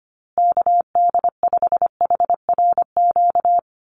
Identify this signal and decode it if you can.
Morse code, heard as a keyed tone, about 25 wpm, 700 Hz. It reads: KD5HRQ
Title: USA Call Signs Repeated 3x 25wpm